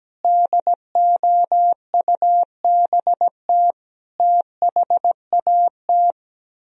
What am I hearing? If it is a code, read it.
Morse code, heard as a keyed tone, about 17 wpm, 700 Hz. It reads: DOUBT THAT
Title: Speed-Racing - Sets of 5 Words 17wpm